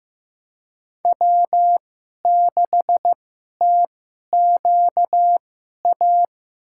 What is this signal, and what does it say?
Morse code, heard as a keyed tone, about 15 wpm, spaced as for 11 wpm, 700 Hz. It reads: W6TQA
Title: USA Call Signs - 5 Characters - 2x Character Spacing - 15wpm